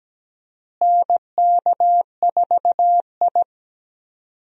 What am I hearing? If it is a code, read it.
Morse code, heard as a keyed tone, about 17 wpm, 700 Hz. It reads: NK4I
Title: USA Call Signs - 4 Characters - Repeated 2x - 17wpm